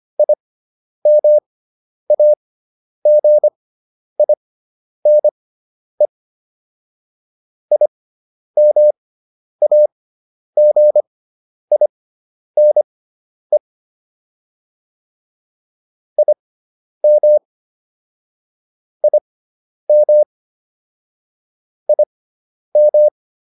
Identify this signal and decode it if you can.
Morse code, heard as a keyed tone, about 25 wpm, spaced as for 10 wpm, 600 Hz. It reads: IMAGINE IMAGINE IM IM IM